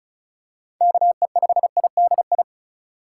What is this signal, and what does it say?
Morse code, heard as a keyed tone, about 34 wpm, 700 Hz. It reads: KE5IDI
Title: USA Call Signs - 2x Character Spacing - 35wpm